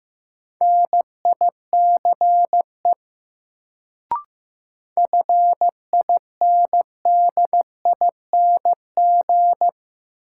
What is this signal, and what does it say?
Morse code, heard as a keyed tone, about 15 wpm, 700 Hz. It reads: NICE FINDING
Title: Sentences from Top 1100 Words - New Word Drill Alt - 15wpm